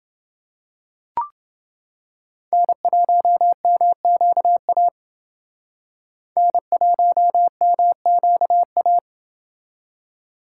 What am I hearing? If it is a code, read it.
Morse code, heard as a keyed tone, about 28 wpm, 700 Hz. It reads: N1MQA N1MQA